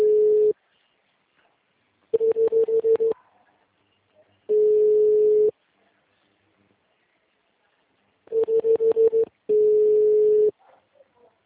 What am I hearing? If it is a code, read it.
Ringback tone (Europe, 425 Hz).